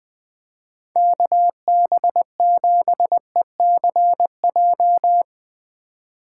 Morse code, heard as a keyed tone, about 20 wpm, 700 Hz. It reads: KB7ECJ
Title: USA Call Signs - 6 Characters - 4x Character Spacing - 20wpm